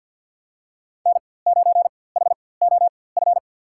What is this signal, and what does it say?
Morse code, heard as a keyed tone, about 53 wpm, spaced as for 24 wpm, 700 Hz. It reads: N9HOF
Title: USA Call Signs - 5 Characters - 4x Character Spacing - 50wpm